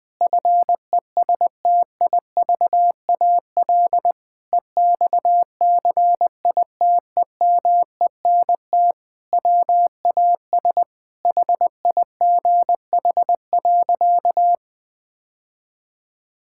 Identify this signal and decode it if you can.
Morse code, heard as a keyed tone, about 21 wpm, 700 Hz. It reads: FESTIVAL EXCITEMENT WAS HIGH.
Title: Sentences with 5-Word Phrases - 20wpm